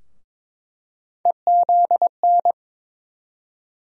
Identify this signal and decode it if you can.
Morse code, heard as a keyed tone, about 22 wpm, 700 Hz. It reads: EZN
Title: USA Call Sign Postfixes - 3 Letters 22wpm